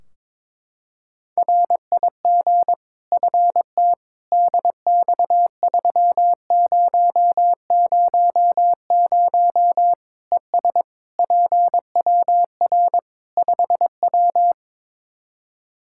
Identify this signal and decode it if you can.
Morse code, heard as a keyed tone, about 22 wpm, 700 Hz. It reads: RIG FT DX3000 ES PWR 5W
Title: Practice QSO - Exchange 2 - Info - 22wpm